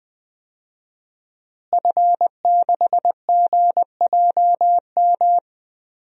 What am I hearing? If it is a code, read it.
Morse code, heard as a keyed tone, about 20 wpm, 700 Hz. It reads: F6GJM
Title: DXCC-100 Call Signs - 5 Characters - Repeated 3x - 20wpm